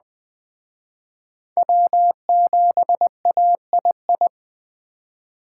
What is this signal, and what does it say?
Morse code, heard as a keyed tone, about 20 wpm, 700 Hz. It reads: W7AII